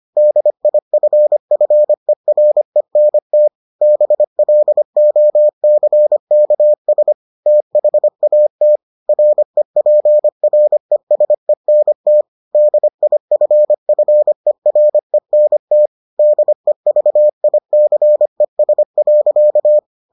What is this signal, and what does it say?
Morse code, heard as a keyed tone, about 25 wpm, 600 Hz. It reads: DIFFERENT BLOCKS THAT REPRESENT DIFFERENT DEVICES.